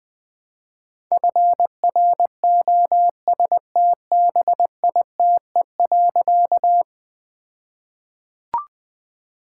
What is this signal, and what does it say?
Morse code, heard as a keyed tone, about 20 wpm, 700 Hz. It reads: FROSTBITE.
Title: All-Age Appropriate Jokes - 20wpm